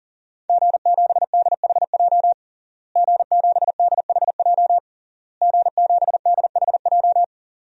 Morse code, heard as a keyed tone, about 40 wpm, 700 Hz. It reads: G7DHJ G7DHJ G7DHJ
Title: DXCC-100 Call Signs - 5 Characters - Repeated 3x - 40wpm